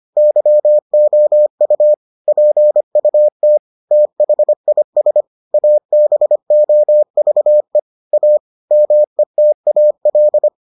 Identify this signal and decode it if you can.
Morse code, heard as a keyed tone, about 25 wpm, 600 Hz. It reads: YOU PUT THIS ABOVE A METAL